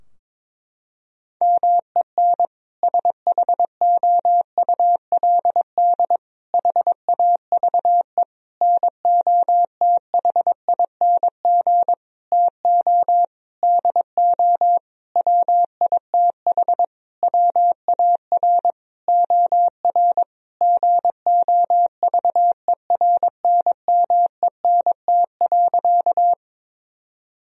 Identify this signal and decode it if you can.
Morse code, heard as a keyed tone, about 22 wpm, 700 Hz. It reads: MEN SHOULD HAVE NOTHING TO DO WITH WAR OR GOVERNMENT.
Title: Sentences from Top 1000 Words - Review Part 3 - 22wpm